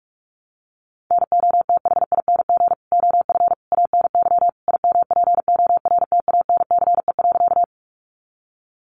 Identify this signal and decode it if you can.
Morse code, heard as a keyed tone, about 44 wpm, 700 Hz. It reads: NOTHING OF ANY IMPORTANCE.
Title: Sentences from Top 1200 Words 45wpm